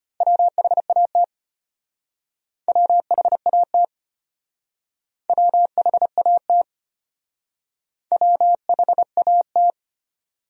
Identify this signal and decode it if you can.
Morse code, heard as a keyed tone, about 32 wpm, 700 Hz. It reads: WHAT WHAT WHAT WHAT